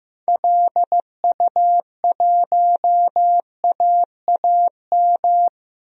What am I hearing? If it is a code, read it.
Morse code, heard as a keyed tone, about 15 wpm, 700 Hz. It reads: LU1AAM
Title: DXCC-100 Call Signs - 6 Characters - 15wpm